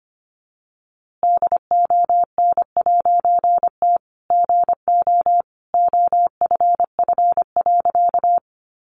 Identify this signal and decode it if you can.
Morse code, heard as a keyed tone, about 25 wpm, 700 Hz. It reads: DON'T GO OFF.
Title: A Sound Of Thunder 25wpm